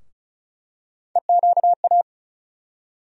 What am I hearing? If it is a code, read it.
Morse code, heard as a keyed tone, about 34 wpm, 700 Hz. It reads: EQA
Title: USA Call Sign Postfixes - 3 Letters 35wpm